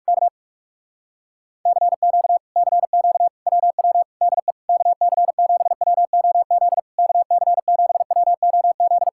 Morse code, heard as a keyed tone, about 46 wpm, 700 Hz. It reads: K CQ CQ WW DE KC7WOZ KC7WOZ
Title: Speed-Racing Common QSO Phrases - 45wpm